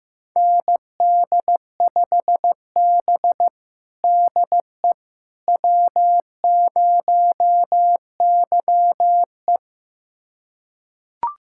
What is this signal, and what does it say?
Morse code, heard as a keyed tone, about 15 wpm, 700 Hz. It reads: ND5B DE W0YE
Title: USA Callsign DE Callsign - 4 Characters - 15wpm